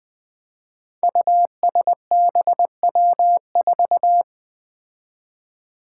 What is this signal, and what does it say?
Morse code, heard as a keyed tone, about 20 wpm, 700 Hz. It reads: USBW4
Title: Five Challenging Letter-Number Combinations 20wpm